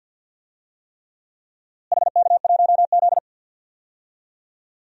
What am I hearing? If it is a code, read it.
Morse code, heard as a keyed tone, about 50 wpm, 700 Hz. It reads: HK1Z
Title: DXCC-100 Call Signs - 4 Characters - Repeated 3x - 50wpm